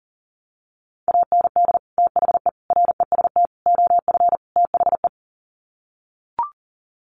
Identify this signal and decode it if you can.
Morse code, heard as a keyed tone, about 40 wpm, 700 Hz. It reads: AND THE REST OF THE
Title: Sets of 5 Words 40wpm